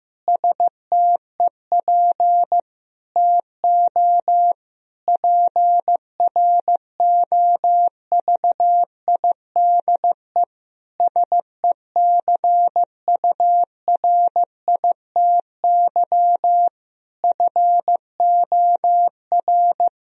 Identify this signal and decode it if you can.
Morse code, heard as a keyed tone, about 16 wpm, 700 Hz. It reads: STEP TO PROVIDE SECURITY FOR
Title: Sentences from Top 1000 Words - Review Part 4 - 15wpm